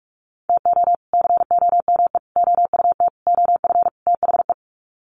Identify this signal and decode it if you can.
Morse code, heard as a keyed tone, about 44 wpm, 700 Hz. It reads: TO COME OUT OF THE
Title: Sets of 5 Words Encore2 45wpm